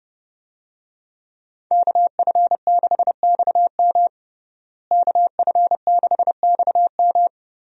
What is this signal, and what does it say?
Morse code, heard as a keyed tone, about 30 wpm, 700 Hz. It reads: KF6XM KF6XM
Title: USA Call Signs - 5 Characters - Repeated 3x - 30wpm